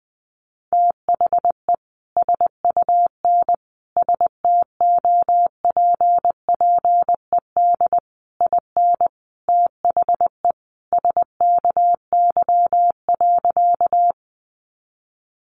Morse code, heard as a keyed tone, about 20 wpm, 700 Hz. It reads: THE SUN STOPPED IN THE SKY.
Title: A Sound Of Thunder 20wpm